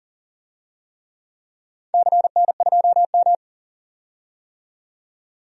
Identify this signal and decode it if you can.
Morse code, heard as a keyed tone, about 38 wpm, 700 Hz. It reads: CN2M
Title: DXCC-100 Call Signs - 4 Characters - 40wpm